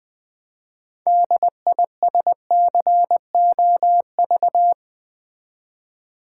Morse code, heard as a keyed tone, about 20 wpm, 700 Hz. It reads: DISCOV
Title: Build A Word - 3 Letters At A Time - 9-Letter Words - 20wpm